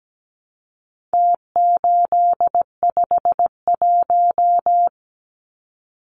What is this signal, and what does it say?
Morse code, heard as a keyed tone, about 17 wpm, 700 Hz. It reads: T851
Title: Numbers - Contest Style - 4-Digit 17wpm